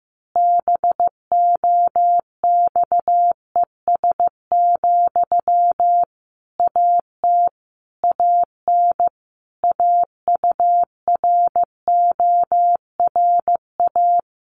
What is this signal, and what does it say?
Morse code, heard as a keyed tone, about 15 wpm, 700 Hz. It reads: BOXES, AT AN AURORA
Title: A Sound Of Thunder 15wpm